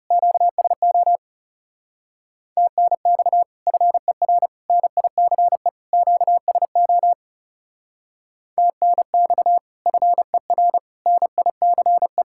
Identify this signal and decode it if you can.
Morse code, heard as a keyed tone, about 35 wpm, 700 Hz. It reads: QSO TNX FER NICE QSO TNX FER NICE
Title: Speed-Racing Common QSO Phrases Encore - 30wpm